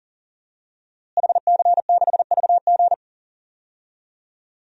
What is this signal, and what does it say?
Morse code, heard as a keyed tone, about 40 wpm, 700 Hz. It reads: HC6VG
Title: DXCC-100 Call Signs - 5 Characters - Repeated 3x - 40wpm